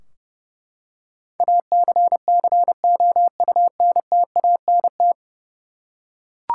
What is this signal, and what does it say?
Morse code, heard as a keyed tone, about 30 wpm, 700 Hz. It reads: ACCOUNTANT
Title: Practice QSO - Exchange 3 - More Info - 30wpm